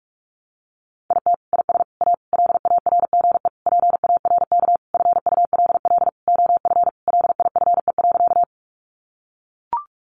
Morse code, heard as a keyed tone, about 44 wpm, 700 Hz. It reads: IT IS A LARGE PARK FULL OF LIFE.
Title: Sentences from Top 1300 Words 45wpm